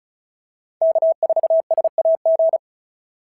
Morse code, heard as a keyed tone, about 36 wpm, 690 Hz. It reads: K4SAG
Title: SOTA QSO Practice 35wpm